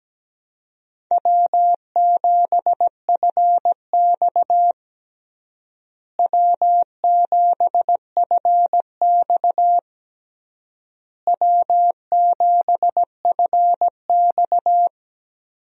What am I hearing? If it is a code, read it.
Morse code, heard as a keyed tone, about 17 wpm, 700 Hz. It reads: W7FX W7FX W7FX
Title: USA Call Signs - 4 Characters - Repeated 3x - 17wpm